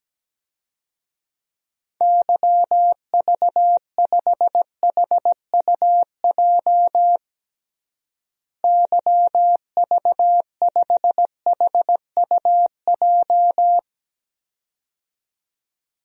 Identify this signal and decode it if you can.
Morse code, heard as a keyed tone, about 17 wpm, 700 Hz. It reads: YV5HUJ YV5HUJ
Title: DXCC-100 Call Signs - 6 Characters - Repeated 2x - 17wpm